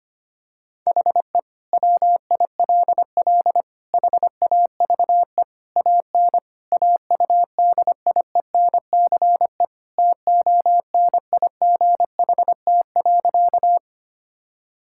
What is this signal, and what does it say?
Morse code, heard as a keyed tone, about 25 wpm, 700 Hz. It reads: HE WILL HAVE AN AUDIENCE TONIGHT.
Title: Sentences from Top 1000 Words - Review Part 5 - 25wpm